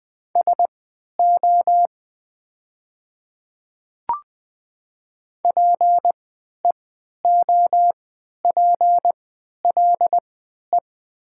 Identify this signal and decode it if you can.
Morse code, heard as a keyed tone, about 20 wpm, spaced as for 11 wpm, 700 Hz. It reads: SO PEOPLE
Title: 3x Character Spacing - Top 100 Words - 20wpm